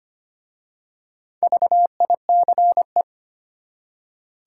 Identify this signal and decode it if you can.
Morse code, heard as a keyed tone, about 25 wpm, 700 Hz. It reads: VICE
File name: practice / 25wpm / Easy Words with Hints - 25wpm